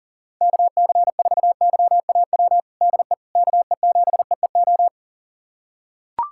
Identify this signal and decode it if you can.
Morse code, heard as a keyed tone, about 40 wpm, 700 Hz. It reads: KC4YAW DE KE7EEO
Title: USA Callsign DE Callsign 40wpm